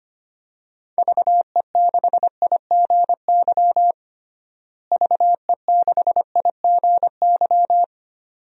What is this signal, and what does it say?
Morse code, heard as a keyed tone, about 25 wpm, 700 Hz. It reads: VE6IGY VE6IGY
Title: DXCC-100 Call Signs - 6 Characters - Repeated 2x - 25wpm